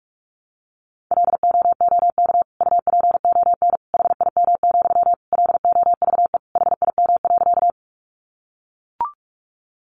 Morse code, heard as a keyed tone, about 44 wpm, 700 Hz. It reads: LOOK UPON HIM, LOVE HIM.
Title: Sentences from Top 1100 Words 45wpm